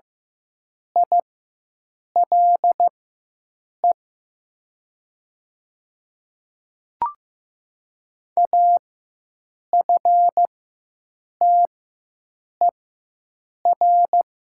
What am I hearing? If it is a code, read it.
Morse code, heard as a keyed tone, about 15 wpm, spaced as for 7 wpm, 700 Hz. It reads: ILE AFTER
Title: 4x Character Spacing - Top 500 Words - 15wpm